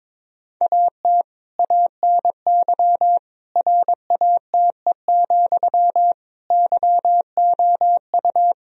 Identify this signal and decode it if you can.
Morse code, heard as a keyed tone, about 22 wpm, 700 Hz. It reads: AT ANY RATE, YOU
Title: Sentences from Top 1000 Words - Review Part 4 - 22wpm